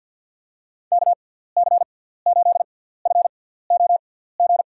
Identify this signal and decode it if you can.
Morse code, heard as a keyed tone, about 51 wpm, spaced as for 17 wpm, 700 Hz. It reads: KC8FOG
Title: USA Call Signs - 6 Characters - 6x Character Spacing - 50wpm